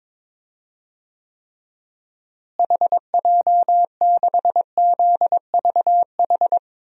Morse code, heard as a keyed tone, about 22 wpm, 700 Hz. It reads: HJ6ZVH
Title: DXCC-100 Call Signs - 6 Characters - Repeated 3x - 22wpm